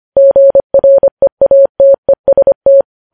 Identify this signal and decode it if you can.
Morse code, heard as a keyed tone, about 25 wpm, 550 Hz. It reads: GREATEST
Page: News Headlines in Morse Code at 25 WPM